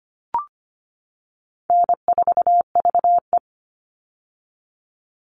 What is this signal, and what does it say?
Morse code, heard as a keyed tone, about 25 wpm, 700 Hz. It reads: N4VE
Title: All Cut Numbers - 4-Digit 25wpm